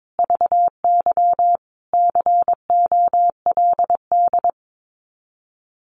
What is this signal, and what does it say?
Morse code, heard as a keyed tone, about 22 wpm, 700 Hz. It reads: VY COLD
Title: Common QSO Phrases 22wpm